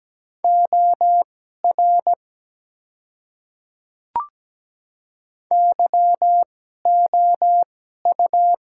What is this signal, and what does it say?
Morse code, heard as a keyed tone, about 17 wpm, spaced as for 12 wpm, 700 Hz. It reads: OR YOU
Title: 2x Character Spacing - Top 100 Words - 17wpm